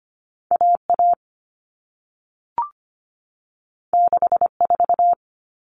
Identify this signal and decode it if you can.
Morse code, heard as a keyed tone, about 25 wpm, 700 Hz. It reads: A A 6 4
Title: All Cut Numbers - 2-Digit 25wpm